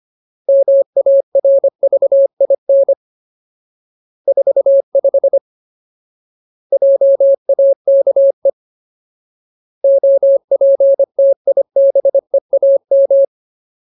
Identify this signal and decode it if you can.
Morse code, heard as a keyed tone, about 25 wpm, 550 Hz. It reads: MARVIN 45 JAKE OPTIBEAM
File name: Ultimate Word List / 25WPM Ultimate